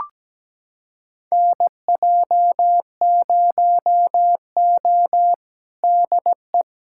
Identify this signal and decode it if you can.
Morse code, heard as a keyed tone, about 17 wpm, 700 Hz. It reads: NJ0O DE
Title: USA Callsign DE Callsign - 4 Characters - 17wpm